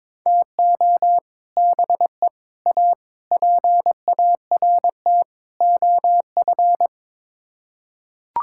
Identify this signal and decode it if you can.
Morse code, heard as a keyed tone, about 22 wpm, 700 Hz. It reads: TO BE A PART OF
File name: Build A Phrase - 5-Word Phrases - 22wpm